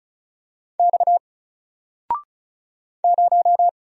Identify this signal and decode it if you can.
Morse code, heard as a keyed tone, about 34 wpm, 700 Hz. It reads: X 0
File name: Single-Letter-Number-35wpm